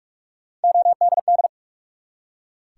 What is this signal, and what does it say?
Morse code, heard as a keyed tone, about 44 wpm, 700 Hz. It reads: ODD
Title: Three Letter Words - 45wpm